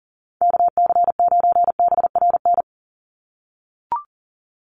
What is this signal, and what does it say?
Morse code, heard as a keyed tone, about 40 wpm, 700 Hz. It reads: KC9BRN
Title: USA Call Signs 40wpm